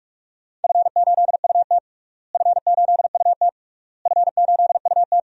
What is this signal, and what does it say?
Morse code, heard as a keyed tone, about 46 wpm, 690 Hz. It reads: F8UT F8UT F8UT
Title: DXCC-100 Call Signs - 4 Characters - Repeated 3x - 45wpm